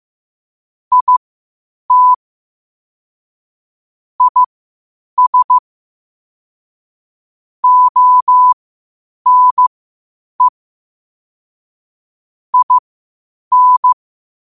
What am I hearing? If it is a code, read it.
Morse code, heard as a keyed tone, about 15 wpm, spaced as for 8 wpm, 1000 Hz. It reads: IT IS ONE IN
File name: qotdfiles / 8wpm